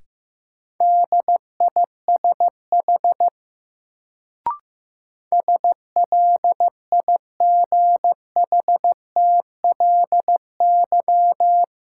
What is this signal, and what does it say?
Morse code, heard as a keyed tone, about 15 wpm, 700 Hz. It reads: DISH SLIGHTLY
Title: Sentences from Top 1300 Words - New Word Drill - 15wpm